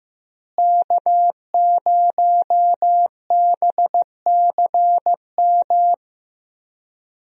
Morse code, heard as a keyed tone, about 15 wpm, 700 Hz. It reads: K0BCM
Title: USA Call Signs - 5 Characters - Repeated 2x - 15wpm